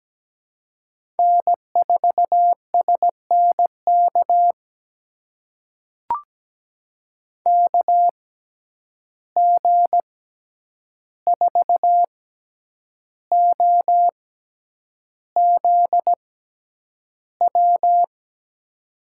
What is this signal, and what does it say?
Morse code, heard as a keyed tone, about 17 wpm, spaced as for 8 wpm, 700 Hz. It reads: N4SNK KG4OZW
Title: USA Call Signs - 6x Character Spacing - 17wpm